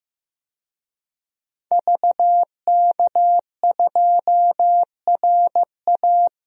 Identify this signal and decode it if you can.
Morse code, heard as a keyed tone, about 15 wpm, 700 Hz. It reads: VK2RA
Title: DXCC-100 Call Signs - 5 Characters - Repeated 3x - 15wpm